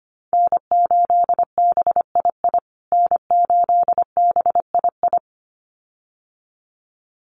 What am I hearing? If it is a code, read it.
Morse code, heard as a keyed tone, about 25 wpm, 700 Hz. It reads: N8BII N8BII
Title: USA Call Signs Repeated 2x 25wpm